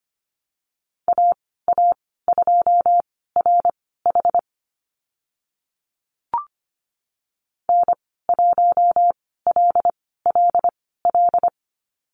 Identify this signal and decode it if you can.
Morse code, heard as a keyed tone, about 25 wpm, spaced as for 16 wpm, 700 Hz. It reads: AA2RH N1LLL
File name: Lesson 15g - Number 5 - Call Signs 25wpm